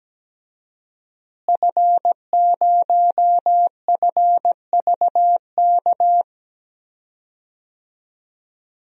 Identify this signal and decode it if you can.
Morse code, heard as a keyed tone, about 17 wpm, 700 Hz. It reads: F0FVK
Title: DXCC-100 Call Signs - 5 Characters - Repeated 3x - 17wpm